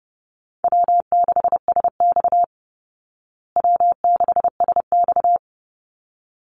Morse code, heard as a keyed tone, about 28 wpm, 700 Hz. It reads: W6SX W6SX
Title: Easy Call Signs Repeated 2x - 30wpm